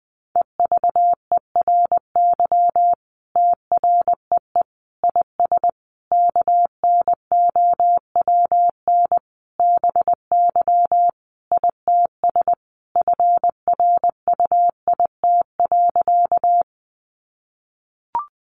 Morse code, heard as a keyed tone, about 20 wpm, 700 Hz. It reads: EVERY TREE IS KNOWN BY ITS FRUIT.
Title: Sentences from Top 1300 Words 20wpm